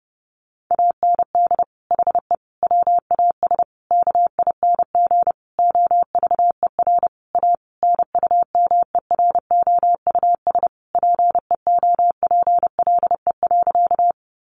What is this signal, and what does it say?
Morse code, heard as a keyed tone, about 30 wpm, 700 Hz. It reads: AND HE WAS KING OVER A NUMEROUS PEOPLE.
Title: Sentences from Top 1400 Words 30wpm